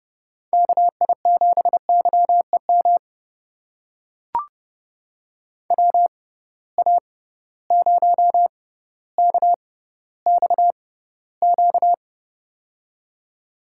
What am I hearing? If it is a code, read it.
Morse code, heard as a keyed tone, about 30 wpm, spaced as for 15 wpm, 700 Hz. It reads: KI7YEM WA0KXQ
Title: USA Call Signs - 6 Characters - 6x Character Spacing - 30wpm